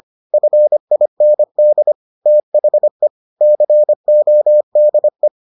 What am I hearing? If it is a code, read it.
Morse code, heard as a keyed tone, about 25 wpm, 600 Hz. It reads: FIND THE CODE